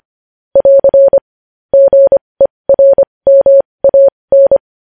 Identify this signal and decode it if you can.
Morse code, heard as a keyed tone, about 25 wpm, spaced as for 20 wpm, 550 Hz. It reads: <AR> GERMAN
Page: news headlines in morse code at 20 wpm